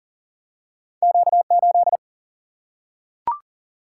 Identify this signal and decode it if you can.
Morse code, heard as a keyed tone, about 39 wpm, 700 Hz. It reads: Q8
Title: Single-Letter Single-Number 40wpm